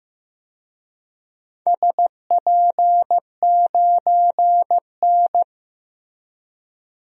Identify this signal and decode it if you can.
Morse code, heard as a keyed tone, about 15 wpm, 700 Hz. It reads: SP9N
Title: DXCC-100 Call Signs - 4 Characters - Repeated 3x - 15wpm